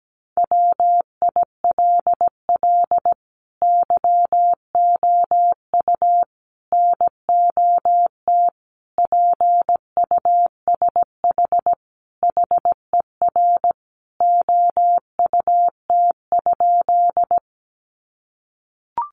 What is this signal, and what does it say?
Morse code, heard as a keyed tone, about 17 wpm, 700 Hz. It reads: WILL YOU NOT PUSH HER OUT?
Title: Sentences from Top 1300 Words 17wpm